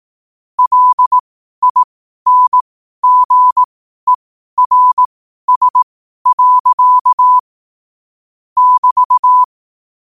Morse code, heard as a keyed tone, about 18 wpm, spaced as for 12 wpm, 1000 Hz. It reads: LINGERS. <BT>
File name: qotdfiles / 12wpm